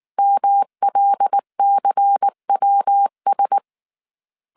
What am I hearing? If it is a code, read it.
Morse code, heard as a keyed tone, about 19 wpm, 800 Hz. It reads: MLCWS
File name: CW / DET / 1 6 1500C S A 18wpm 2255